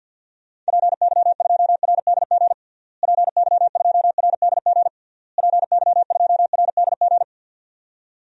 Morse code, heard as a keyed tone, about 51 wpm, 700 Hz. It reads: PY2RDG PY2RDG PY2RDG
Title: DXCC-100 Call Signs - 6 Characters - Repeated 3x - 50wpm